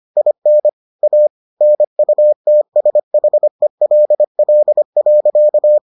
Morse code, heard as a keyed tone, about 25 wpm, 600 Hz. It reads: IN A NUTSHELL.